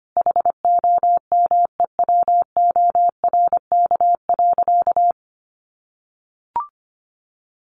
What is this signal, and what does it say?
Morse code, heard as a keyed tone, about 25 wpm, 700 Hz. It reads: HOMEWORK.
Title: Additional Easy Phrases 25wpm